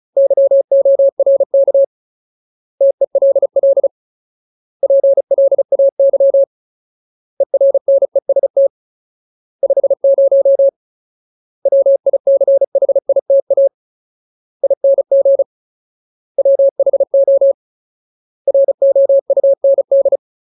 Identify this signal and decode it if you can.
Morse code, heard as a keyed tone, about 34 wpm, 550 Hz. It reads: YORK TELL PLAY ERNEST 50 WICHITA ING WHO ROUND
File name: Ultimate Word List / 35WPM Ultimate